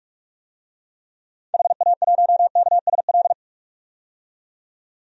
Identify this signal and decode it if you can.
Morse code, heard as a keyed tone, about 41 wpm, 700 Hz. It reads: HA1KSL
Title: DXCC-100 Call Signs - 6 Characters - 45wpm